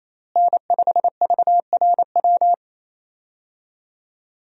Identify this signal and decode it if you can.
Morse code, heard as a keyed tone, about 28 wpm, 700 Hz. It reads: N5VRW
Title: USA Call Signs - 5 Characters - Repeated 3x - 28wpm